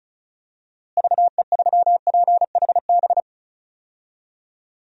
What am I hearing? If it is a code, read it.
Morse code, heard as a keyed tone, about 36 wpm, 700 Hz. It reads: VE3PHB
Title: DXCC-100 Call Signs - 6 Characters - Repeated 3x - 35wpm